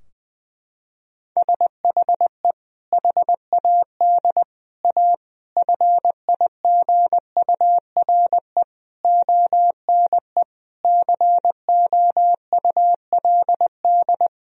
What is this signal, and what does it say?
Morse code, heard as a keyed tone, about 20 wpm, 700 Hz. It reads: SHE HAD A FIGURE ONE COULD
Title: Sentences from Top 1000 Words - Review Part 2 - 20wpm